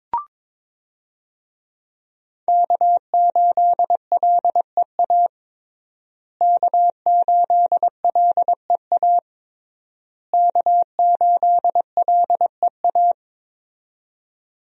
Morse code, heard as a keyed tone, about 22 wpm, 700 Hz. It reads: K8LEA K8LEA K8LEA
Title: USA Call Signs - 5 Characters - Repeated 3x - 22wpm